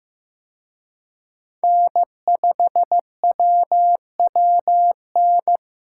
Morse code, heard as a keyed tone, about 18 wpm, 700 Hz. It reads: N5WWN
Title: Speed Racing - USA Call Signs 15wpm